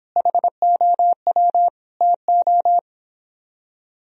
Morse code, heard as a keyed tone, about 31 wpm, 700 Hz. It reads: HOW TO